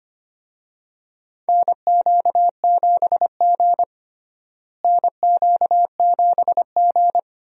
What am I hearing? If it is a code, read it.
Morse code, heard as a keyed tone, about 25 wpm, 700 Hz. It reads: NQ7G NQ7G
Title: USA Call Signs - 4 Characters - Repeated 3x - 25wpm